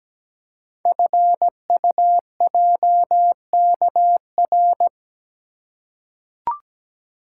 Morse code, heard as a keyed tone, about 17 wpm, 700 Hz. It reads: FUJKR
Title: Code Groups - 5 Letters - 17wpm